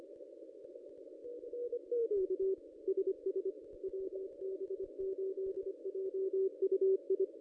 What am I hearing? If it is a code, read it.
Morse code, heard as a keyed tone, about 26 wpm, 380 Hz. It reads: K SS WB8JUI
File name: contests / ss-cw-k5tr-op-n5rz -1400